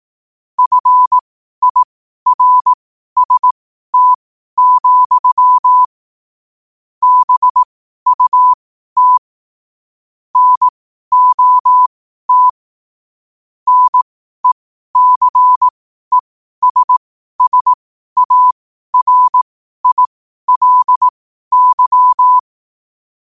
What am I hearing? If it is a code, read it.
Morse code, heard as a keyed tone, about 18 wpm, spaced as for 12 wpm, 1000 Hz. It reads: FIRST, BUT NOT NECESSARILY
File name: qotdfiles / 12wpm